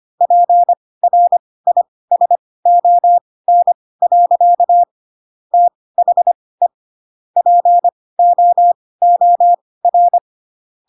Morse code, heard as a keyed tone, about 25 wpm, spaced as for 18 wpm, 700 Hz. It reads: PRISON. THE POOR
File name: Pinocchio ch3 12 25 ik1hge